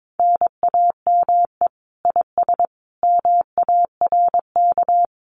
Morse code, heard as a keyed tone, about 22 wpm, 700 Hz. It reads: NAME IS MARK
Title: Common QSO Phrases Encore 22wpm